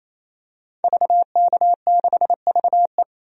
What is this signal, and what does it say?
Morse code, heard as a keyed tone, about 28 wpm, 700 Hz. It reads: VK6VE
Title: DXCC-100 Call Signs - 5 Characters - 28wpm